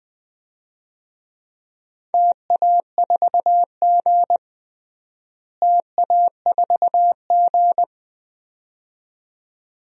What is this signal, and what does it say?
Morse code, heard as a keyed tone, about 20 wpm, 700 Hz. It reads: TA4G TA4G
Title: DXCC-100 Call Signs - 4 Characters - Repeated 2x - 20wpm